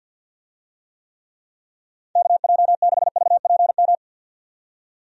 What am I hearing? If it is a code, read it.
Morse code, heard as a keyed tone, about 46 wpm, 700 Hz. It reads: KJ6VPM
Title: USA Call Signs - 6 Characters - 50wpm